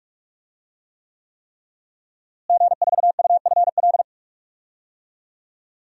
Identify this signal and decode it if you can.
Morse code, heard as a keyed tone, about 43 wpm, 700 Hz. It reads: G4UFL
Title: DXCC-100 Call Signs - 5 Characters - Repeated 3x - 45wpm